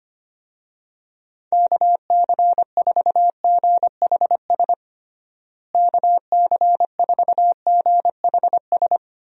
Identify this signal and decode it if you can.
Morse code, heard as a keyed tone, about 25 wpm, 700 Hz. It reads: KC4GHS KC4GHS
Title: USA Call Signs - 6 Characters - Repeated 2x - 25wpm